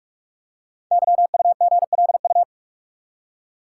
Morse code, heard as a keyed tone, about 44 wpm, 700 Hz. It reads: YUGLU
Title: Code Groups - 5 Letter:Numbers - 45wpm